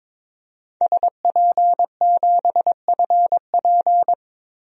Morse code, heard as a keyed tone, about 22 wpm, 700 Hz. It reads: SP7FP
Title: DXCC-100 Call Signs - 5 Characters - 22wpm